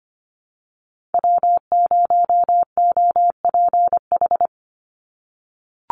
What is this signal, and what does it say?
Morse code, heard as a keyed tone, about 25 wpm, 700 Hz. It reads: W0OPH
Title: USA Call Signs 25wpm